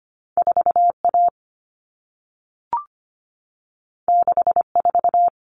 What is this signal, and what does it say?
Morse code, heard as a keyed tone, about 25 wpm, 700 Hz. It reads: 4A 64
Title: Common Cut Numbers - 2-Digit 25wpm